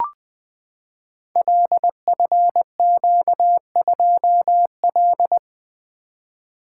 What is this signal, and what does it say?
Morse code, heard as a keyed tone, about 20 wpm, 700 Hz. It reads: LFQ2L
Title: Code Groups - 5 Letter:Numbers - 20wpm